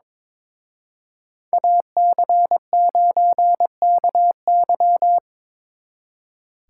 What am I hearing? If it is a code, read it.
Morse code, heard as a keyed tone, about 26 wpm, 700 Hz. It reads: AC9KY
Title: Speed Racing - USA 2x2 Call Signs 22wpm